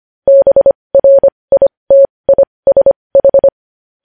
Morse code, heard as a keyed tone, about 25 wpm, spaced as for 20 wpm, 550 Hz. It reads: BRITISH
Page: news-headlines-in-morse-code-at-20-wpm